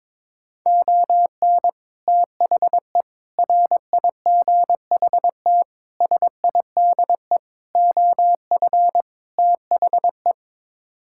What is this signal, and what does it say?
Morse code, heard as a keyed tone, about 22 wpm, 700 Hz. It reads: ON THE RIGHT SIDE OF THE
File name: practice / 22wpm / Sets of 6 Words - 22wpm